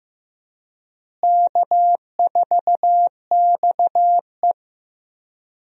Morse code, heard as a keyed tone, about 15 wpm, 700 Hz. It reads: K4XE
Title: USA Call Signs - 4 Characters - Repeated 2x - 15wpm